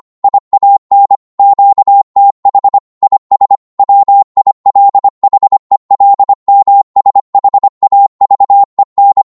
Morse code, heard as a keyed tone, about 25 wpm, 800 Hz. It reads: IAN QTH IS WILHELMSHAVEN
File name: Morse / qso-25wpm